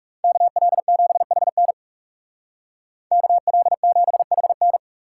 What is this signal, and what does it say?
Morse code, heard as a keyed tone, about 39 wpm, 700 Hz. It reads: KL7HN KL7HN